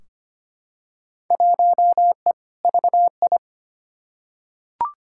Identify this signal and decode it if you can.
Morse code, heard as a keyed tone, about 25 wpm, 700 Hz. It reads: 1E VI
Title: Field Day Exchanges - 25wpm